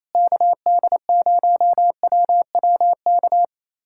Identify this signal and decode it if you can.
Morse code, heard as a keyed tone, about 28 wpm, 700 Hz. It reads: KD0WWK
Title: USA Callsign DE Callsign 28wpm